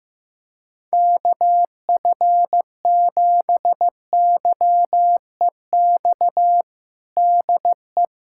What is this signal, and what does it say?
Morse code, heard as a keyed tone, about 15 wpm, 700 Hz. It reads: KF7YEX DE
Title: USA Callsign DE Callsign 15wpm